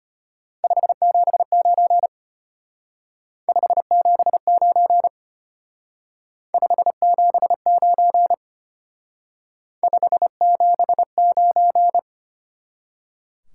Morse code, heard as a keyed tone, about 32 wpm, 700 Hz. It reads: 579 579 579 579